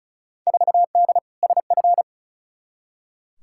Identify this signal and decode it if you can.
Morse code, heard as a keyed tone, about 36 wpm, 700 Hz. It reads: 4D SF
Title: Field Day Exchanges Encore - 35wpm